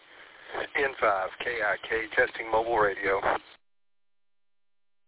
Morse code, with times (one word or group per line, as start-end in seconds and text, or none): none